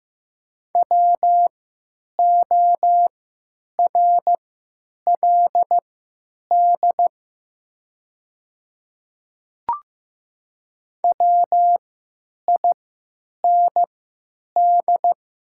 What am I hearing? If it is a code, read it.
Morse code, heard as a keyed tone, about 15 wpm, spaced as for 9 wpm, 700 Hz. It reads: WORLD WIND
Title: 3x Character Spacing - Top 500 Words - 15wpm